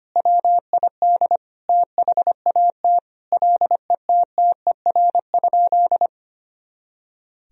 Morse code, heard as a keyed tone, about 25 wpm, 700 Hz. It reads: WID THAT LETTER?
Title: Sentences from Top 300 Words and Abbreviations 25wpm